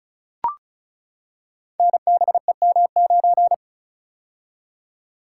Morse code, heard as a keyed tone, about 36 wpm, 700 Hz. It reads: NBEM9
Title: Code Groups - 5 Letter:Numbers - 35wpm